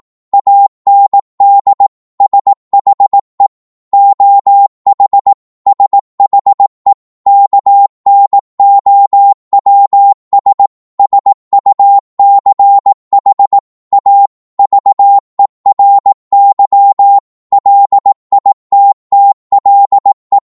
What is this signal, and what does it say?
Morse code, heard as a keyed tone, about 18 wpm, 800 Hz. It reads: AND SHE OH SHE KNOWS SUCH A VERY LITTLE